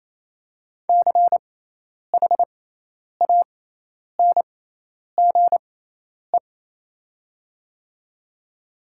Morse code, heard as a keyed tone, about 27 wpm, spaced as for 10 wpm, 700 Hz. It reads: CHANGE
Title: 6x Character Spacing - Top 200 Words - 28wpm